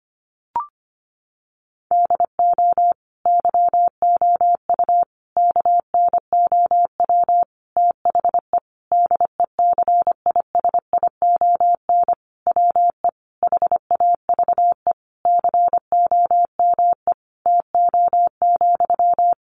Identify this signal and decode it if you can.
Morse code, heard as a keyed tone, about 25 wpm, 700 Hz. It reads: DO YOU KNOW THE DECISION WE HAVE COME TO,
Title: Sentences from Top 1000 Words 25wpm